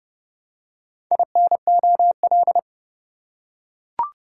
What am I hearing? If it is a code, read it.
Morse code, heard as a keyed tone, about 30 wpm, 700 Hz. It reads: INOL